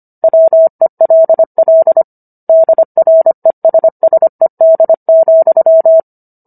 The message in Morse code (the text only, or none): WELL DRESSED,